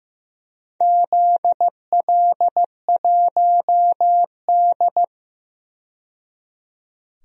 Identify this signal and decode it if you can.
Morse code, heard as a keyed tone, about 15 wpm, 700 Hz. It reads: ZL1D
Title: DXCC-100 Call Signs - 4 Characters - 15wpm